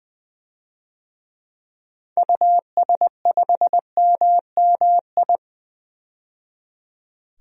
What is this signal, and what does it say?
Morse code, heard as a keyed tone, about 20 wpm, 700 Hz. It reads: US5MMI
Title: DXCC-100 Call Signs - 6 Characters - 20wpm